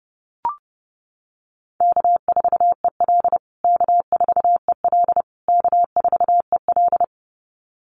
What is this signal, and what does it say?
Morse code, heard as a keyed tone, about 30 wpm, 700 Hz. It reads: K4EL K4EL K4EL
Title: USA Call Signs Repeated 3x 30wpm